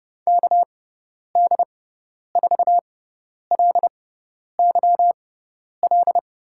Morse code, heard as a keyed tone, about 31 wpm, spaced as for 10 wpm, 700 Hz. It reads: KD4LYL
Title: USA Call Signs - 6 Characters - 6x Character Spacing - 30wpm